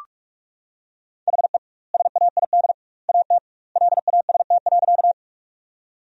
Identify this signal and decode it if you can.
Morse code, heard as a keyed tone, about 46 wpm, 700 Hz. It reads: HE SAID AT LAST.
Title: 2x Word-Spacing - Sentences from Top 300 Words - 45wpm